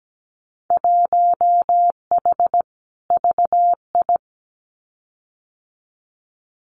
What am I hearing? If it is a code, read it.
Morse code, heard as a keyed tone, about 17 wpm, 700 Hz. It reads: 1H VI
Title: Winter Field Day Exchanges Encore 17wpm